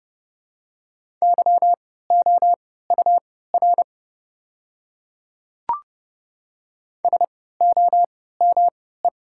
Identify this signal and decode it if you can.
Morse code, heard as a keyed tone, about 30 wpm, spaced as for 17 wpm, 700 Hz. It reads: YOUR SOME
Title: 3x Character Spacing - Top 100 Words - 30wpm